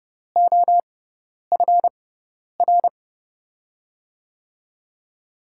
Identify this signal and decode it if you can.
Morse code, heard as a keyed tone, about 31 wpm, spaced as for 10 wpm, 700 Hz. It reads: OFR
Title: USA Call Signs - 6x Character Spacing - 30wpm